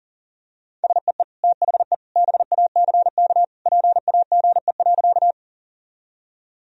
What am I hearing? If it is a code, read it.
Morse code, heard as a keyed tone, about 40 wpm, 700 Hz. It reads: SEE THE BACK PAGE.
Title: Sentences from Top 1000 Words - Review Part 1 - 40wpm